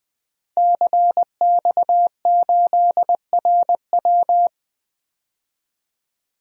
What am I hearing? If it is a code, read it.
Morse code, heard as a keyed tone, about 20 wpm, 700 Hz. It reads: CX8RW
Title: DXCC-100 Call Signs - 5 Characters - 20wpm